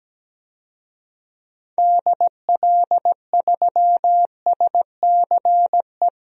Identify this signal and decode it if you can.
Morse code, heard as a keyed tone, about 17 wpm, 700 Hz. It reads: DL3SCE
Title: DXCC-100 Call Signs - 6 Characters - Repeated 3x - 17wpm